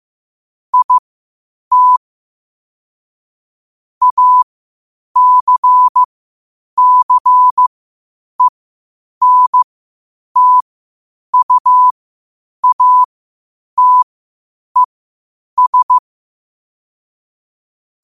Morse code, heard as a keyed tone, about 15 wpm, spaced as for 8 wpm, 1000 Hz. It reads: IT ACCENTUATES
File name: qotdfiles / 8wpm